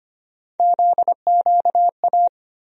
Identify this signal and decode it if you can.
Morse code, heard as a keyed tone, about 25 wpm, 700 Hz. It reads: ZQA
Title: USA Call Sign Postfixes - 3 Letters 25wpm